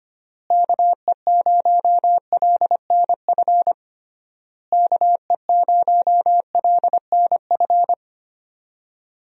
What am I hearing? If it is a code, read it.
Morse code, heard as a keyed tone, about 25 wpm, 700 Hz. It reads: KE0LNF KE0LNF
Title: USA Call Signs - 6 Characters - Repeated 3x - 25wpm